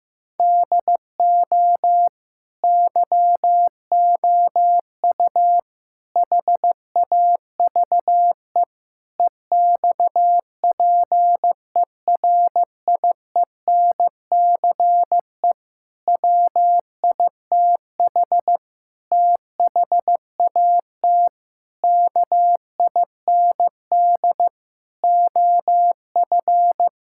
Morse code, heard as a keyed tone, about 15 wpm, 700 Hz. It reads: DO YOU HAVE EXPERIENCE WITH THAT KIND OF
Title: Sentences from Top 1000 Words - Review Part 5 - 15wpm